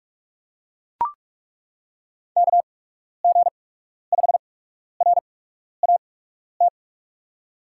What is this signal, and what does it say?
Morse code, heard as a keyed tone, about 43 wpm, spaced as for 12 wpm, 700 Hz. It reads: KG5RAT
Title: USA Call Signs - 6 Characters - 8x Character Spacing - 45wpm